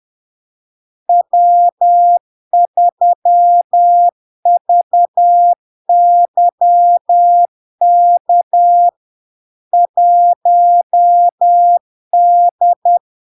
Morse code, heard as a keyed tone, about 10 wpm, 700 Hz. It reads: W3VYK 1D
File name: LARK-FD-2022-Log-in-CW-10wpm-700-Hz